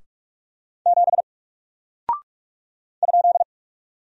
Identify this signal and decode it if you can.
Morse code, heard as a keyed tone, about 45 wpm, 700 Hz. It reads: 7 ?
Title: Single Character (Letter:Number:Punctuation:Prosign) Alt 45wpm